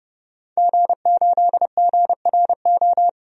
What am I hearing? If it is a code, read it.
Morse code, heard as a keyed tone, about 30 wpm, 700 Hz. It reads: G8GRO
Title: DXCC-100 Call Signs - 5 Characters - 30wpm